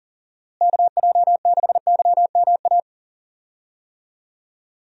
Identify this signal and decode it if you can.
Morse code, heard as a keyed tone, about 39 wpm, 700 Hz. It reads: KJ6YMA
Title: USA Call Signs - 6 Characters - Repeated 3x - 40wpm